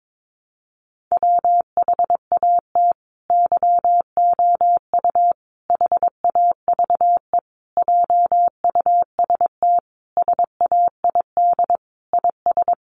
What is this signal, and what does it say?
Morse code, heard as a keyed tone, about 22 wpm, 700 Hz. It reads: WHAT YOU HAVE JUST SAID IS
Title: Sentences from Top 200 Words 22wpm